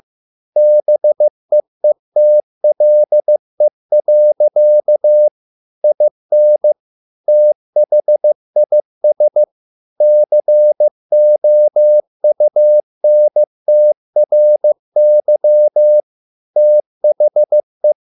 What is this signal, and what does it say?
Morse code, heard as a keyed tone, about 15 wpm, 600 Hz. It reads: BEETLE. IN THIS COUNTRY THE